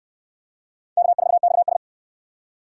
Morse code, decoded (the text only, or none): K4XS